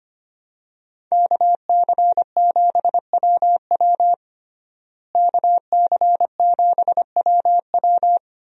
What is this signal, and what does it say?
Morse code, heard as a keyed tone, about 25 wpm, 700 Hz. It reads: KC7WW KC7WW
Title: USA Call Signs - 5 Characters - Repeated 3x - 25wpm